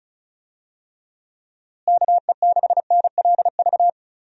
Morse code, heard as a keyed tone, about 34 wpm, 700 Hz. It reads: KE6NLV
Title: USA Call Signs - 6 Characters - Repeated 3x - 35wpm